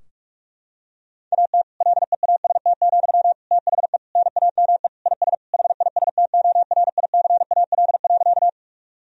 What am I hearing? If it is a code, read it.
Morse code, heard as a keyed tone, about 46 wpm, 700 Hz. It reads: AT LEAST, THE NAME IS HISTORICAL.
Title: Sentences from Top 1000 Words - Review Part 4 - 45wpm